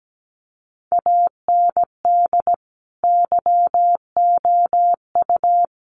Morse code, heard as a keyed tone, about 17 wpm, 700 Hz. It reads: AND YOU
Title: Sets of 5 Words Encore2 17wpm